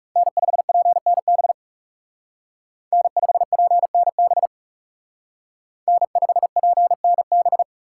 Morse code, heard as a keyed tone, about 40 wpm, 700 Hz. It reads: N5PNB N5PNB N5PNB